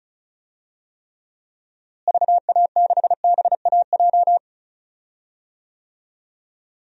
Morse code, heard as a keyed tone, about 34 wpm, 700 Hz. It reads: VA6BAJ
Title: DXCC-100 Call Signs - 6 Characters - Repeated 3x - 35wpm